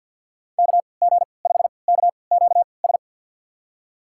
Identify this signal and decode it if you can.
Morse code, heard as a keyed tone, about 51 wpm, spaced as for 28 wpm, 700 Hz. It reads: KG5KQS
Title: USA Call Signs - 6 Characters - 3x Character Spacing - 50wpm